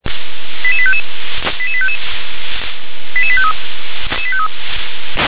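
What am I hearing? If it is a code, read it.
Five-tone selective call (ZVEI), heard as quick runs of five tones, tones of about 70 ms. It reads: 88855 88855 88853
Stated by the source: ZVEI five-tone selective call received via SDR.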